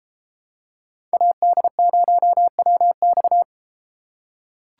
Morse code, heard as a keyed tone, about 33 wpm, 700 Hz. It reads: AD0WX